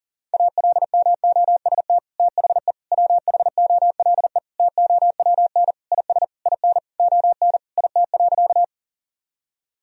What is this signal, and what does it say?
Morse code, heard as a keyed tone, about 40 wpm, 700 Hz. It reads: ALMOST THE WHOLE TOWN IS IN ON IT.
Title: Sentences from Top 1000 Words - Review Part 4 - 40wpm